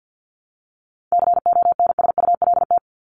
Morse code, heard as a keyed tone, about 48 wpm, 700 Hz. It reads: CONSULT
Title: Top 1751-2000 Words 50wpm